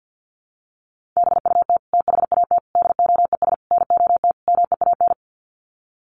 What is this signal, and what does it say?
Morse code, heard as a keyed tone, about 50 wpm, 700 Hz. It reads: BUT THAT DOES NOT MEAN
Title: Sets of 5 Words 50wpm